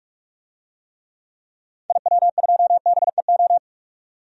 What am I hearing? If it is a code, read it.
Morse code, heard as a keyed tone, about 46 wpm, 700 Hz. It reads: IW2BEO
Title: DXCC-100 Call Signs - 6 Characters - 45wpm